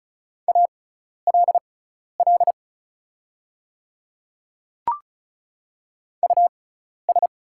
Morse code, heard as a keyed tone, about 36 wpm, spaced as for 12 wpm, 700 Hz. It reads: ALL US
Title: 6x Character Spacing - Top 200 Words - 35wpm